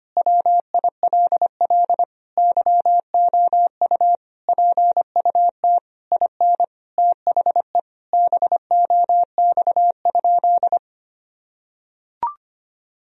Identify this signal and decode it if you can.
Morse code, heard as a keyed tone, about 25 wpm, 700 Hz. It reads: WILL YOU PUT IN THE BOX?
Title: Sentences from Top 1000 Words - Review Part 2 - 25wpm